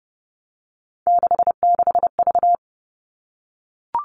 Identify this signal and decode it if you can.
Morse code, heard as a keyed tone, about 30 wpm, 700 Hz. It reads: 66V
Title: Three Challenging Letter-Number Combinations 30wpm